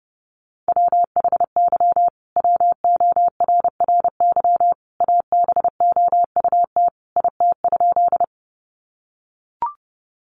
Morse code, heard as a keyed tone, about 30 wpm, 700 Hz. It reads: WHY WORRY ABOUT IT?
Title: Sentences from Top 1100 Words 30wpm